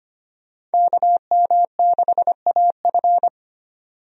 Morse code, heard as a keyed tone, about 25 wpm, 700 Hz. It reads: KM6AF
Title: USA Call Signs - 5 Characters - 25wpm